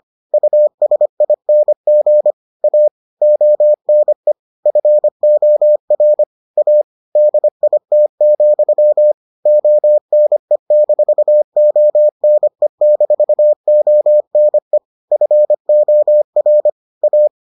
Morse code, heard as a keyed tone, about 25 wpm, 600 Hz. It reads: USING A ONE FOR A DIT, ONE-ONE-ONE FOR A